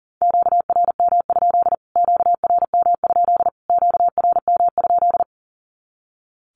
Morse code, heard as a keyed tone, about 40 wpm, 700 Hz. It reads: QRM? QRM? QRM?
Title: Q-Codes Repeated 3x 40wpm